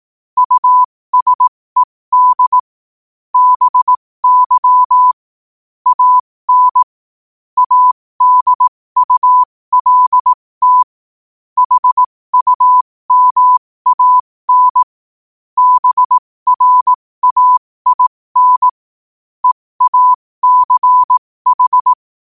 Morse code, heard as a keyed tone, about 18 wpm, spaced as for 15 wpm, 1000 Hz. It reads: USED BY AN ADULT HUMAN BRAIN EACH
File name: qotdfiles / 15wpm